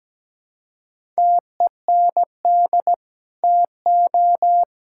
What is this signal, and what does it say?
Morse code, heard as a keyed tone, about 17 wpm, 700 Hz. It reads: TEND TO
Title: Sentences from Top 1000 Words - Review Part 5 - 17wpm